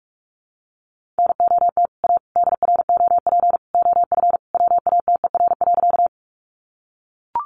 Morse code, heard as a keyed tone, about 44 wpm, 700 Hz. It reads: NOT A DROP OF WATER.
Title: Sentences from Top 1400 Words 45wpm